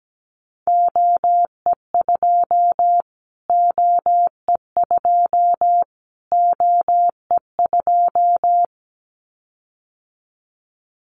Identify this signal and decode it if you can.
Morse code, heard as a keyed tone, about 17 wpm, 700 Hz. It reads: OE2 OE2 OE2
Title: Callsign Prefixes for Top 25 DXCC - Repeated 3x - 17wpm